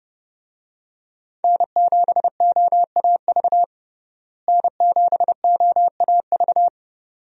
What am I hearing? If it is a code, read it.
Morse code, heard as a keyed tone, about 32 wpm, 700 Hz. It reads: N7OAV N7OAV
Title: USA Call Signs - 5 Characters - Repeated 2x - 30wpm